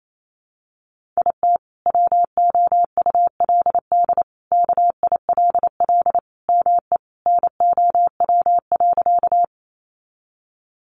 Morse code, heard as a keyed tone, about 28 wpm, 700 Hz. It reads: IT WOULD KILL ME NOW.
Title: Sentences from Top 1400 Words 28wpm